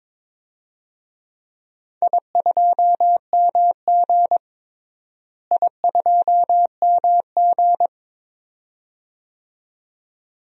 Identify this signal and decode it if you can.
Morse code, heard as a keyed tone, about 22 wpm, 700 Hz. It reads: I2MG I2MG
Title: DXCC-100 Call Signs - 4 Characters - Repeated 2x - 22wpm